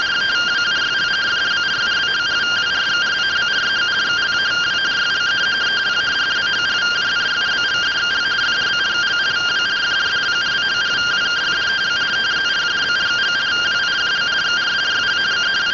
Two-tone teleprinter signal, about 170 Hz shift.